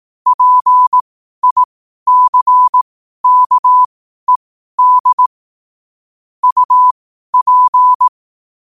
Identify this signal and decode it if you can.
Morse code, heard as a keyed tone, about 18 wpm, spaced as for 12 wpm, 1000 Hz. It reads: PICKED UP
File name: qotdfiles / 12wpm